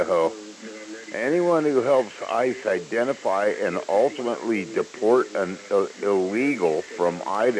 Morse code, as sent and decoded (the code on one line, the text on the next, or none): none